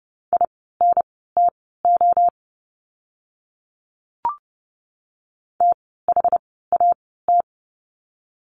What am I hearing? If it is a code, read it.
Morse code, heard as a keyed tone, about 30 wpm, spaced as for 17 wpm, 700 Hz. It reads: INTO THAT
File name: course / 30:10wpm / 115 Lesson 19f - Letter w - Words 30wpm